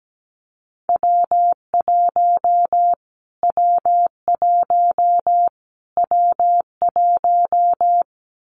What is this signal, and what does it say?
Morse code, heard as a keyed tone, about 17 wpm, 700 Hz. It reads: W1 W1 W1
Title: Callsign Prefixes for Top 25 DXCC - Repeated 3x - 17wpm